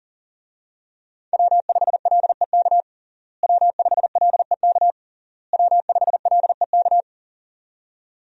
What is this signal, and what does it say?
Morse code, heard as a keyed tone, about 40 wpm, 700 Hz. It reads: W5LEK W5LEK W5LEK
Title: USA Call Signs - 5 Characters - Repeated 3x - 40wpm